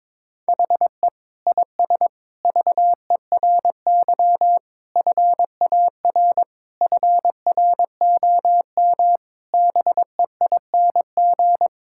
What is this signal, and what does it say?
Morse code, heard as a keyed tone, about 22 wpm, 700 Hz. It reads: HE IS VERY FAR FROM BEING
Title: Sentences from Top 1000 Words - Review Part 6 - 22wpm